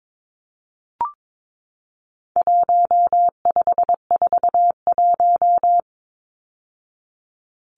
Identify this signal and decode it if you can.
Morse code, heard as a keyed tone, about 22 wpm, 700 Hz. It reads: 1541
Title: Numbers - 4-Digit 22wpm